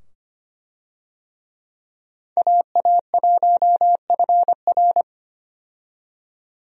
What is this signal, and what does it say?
Morse code, heard as a keyed tone, about 25 wpm, 700 Hz. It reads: AA1FR
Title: USA Call Signs - 5 Characters - 25wpm